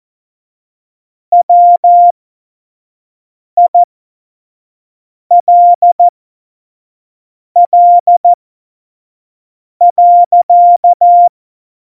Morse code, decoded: WILL.

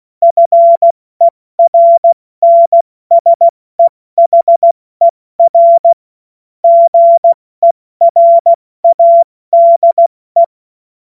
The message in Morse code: FERNSEHER GERADE

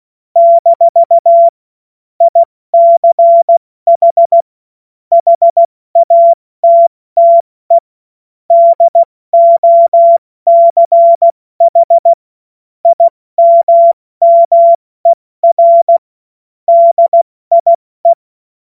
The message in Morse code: - ICH HATTE DOCH IMMER DIE

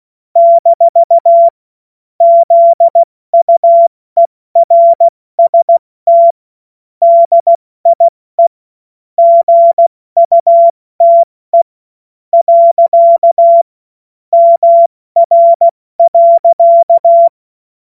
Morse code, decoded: - ZUERST DIE GUTE. MR.

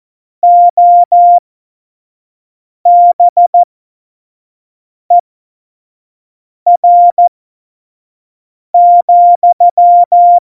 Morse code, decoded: OBER,

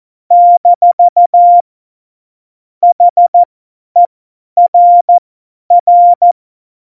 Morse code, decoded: - HERR

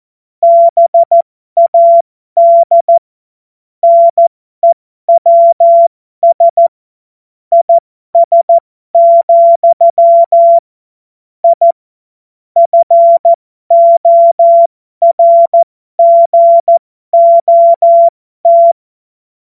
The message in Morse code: BAD NEWS IS, I FORGOT

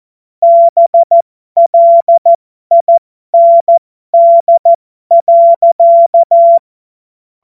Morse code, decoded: BLIND.

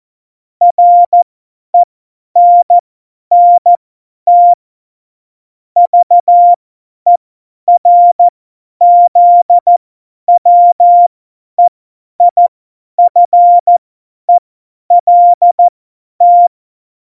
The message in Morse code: RENNT VERZWEIFELT

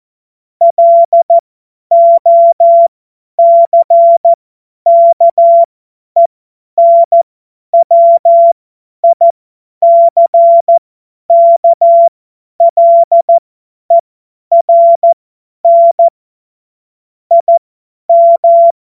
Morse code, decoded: LOCKENWICKLERN IM

